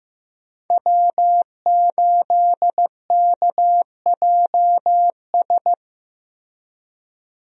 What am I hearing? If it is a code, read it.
Morse code, heard as a keyed tone, about 15 wpm, 700 Hz. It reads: W8KJS
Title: USA Call Signs - 5 Characters - Repeated 3x - 15wpm